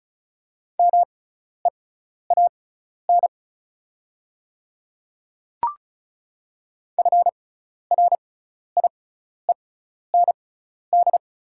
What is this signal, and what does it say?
Morse code, heard as a keyed tone, about 36 wpm, spaced as for 12 wpm, 690 Hz. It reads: MEAN FRIEND
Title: 6x Character Spacing - Top 300 Words - 35wpm